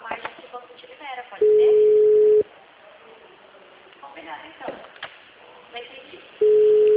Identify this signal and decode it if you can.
Ringback tone (Europe, 425 Hz), with speech.
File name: out- -2030- - - .4678